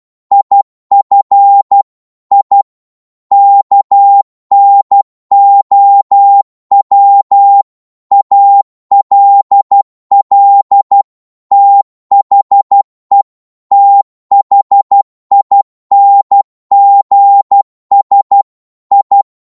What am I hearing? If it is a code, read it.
Morse code, heard as a keyed tone, about 12 wpm, 800 Hz. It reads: IF I KNOW ALL THE THINGS I
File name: 12wpm